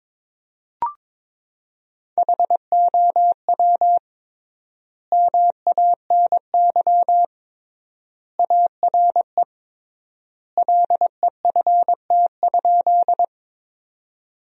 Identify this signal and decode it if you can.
Morse code, heard as a keyed tone, about 22 wpm, 700 Hz. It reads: HOW MANY ARE LEFT?
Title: 3x Word-Spacing - Sentences from Top 300 Words - 22wpm